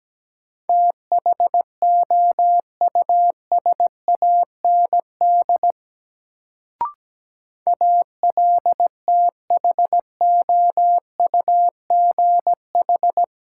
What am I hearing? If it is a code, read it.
Morse code, heard as a keyed tone, about 17 wpm, 700 Hz. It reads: THOUSAND ALTHOUGH
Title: Sentences from Top 600 Words - New Word Drill - 17wpm